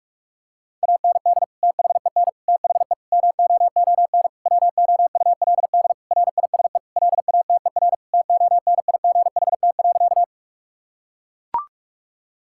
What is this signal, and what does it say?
Morse code, heard as a keyed tone, about 41 wpm, 700 Hz. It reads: AND THEN THE MOON WOULD RISE LATER TONIGHT.
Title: Sentences from Top 1000 Words - Review Part 6 - 45wpm